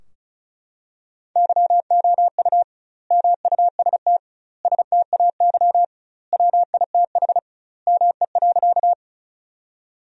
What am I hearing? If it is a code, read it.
Morse code, heard as a keyed tone, about 36 wpm, 700 Hz. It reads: YOU MUST STAY WITH ME.
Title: 2x Word-Spacing - Sentences from Top 500 Words - 35wpm